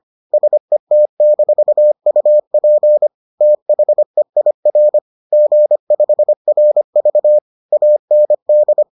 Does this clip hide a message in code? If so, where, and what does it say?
Yes: Morse code from 0.3 to 8.8 s, heard as a keyed tone, about 25 wpm, 600 Hz. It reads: SET-UP THEIR G5RV AND